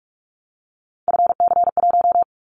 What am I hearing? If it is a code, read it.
Morse code, heard as a keyed tone, about 46 wpm, 700 Hz. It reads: FC1
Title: Callsign Prefixes for Top 50 DXCC - 45wpm